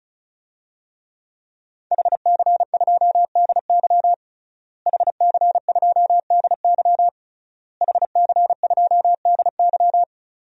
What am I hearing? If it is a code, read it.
Morse code, heard as a keyed tone, about 36 wpm, 700 Hz. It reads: HC2DY HC2DY HC2DY
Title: DXCC-100 Call Signs - 5 Characters - Repeated 3x - 35wpm